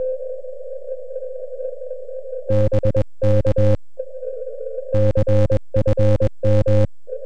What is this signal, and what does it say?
Morse code, heard as a keyed tone, about 21 wpm, 2000 Hz. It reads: BK CFM
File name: KL7RA-BRQ-160cw